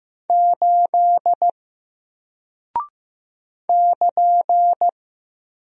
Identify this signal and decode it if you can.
Morse code, heard as a keyed tone, about 15 wpm, 700 Hz. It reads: 8 <KN>
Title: Single Character (Letter:Number:Punctuation:Prosign) Alt 15wpm